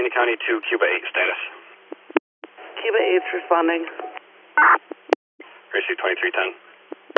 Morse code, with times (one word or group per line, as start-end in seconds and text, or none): none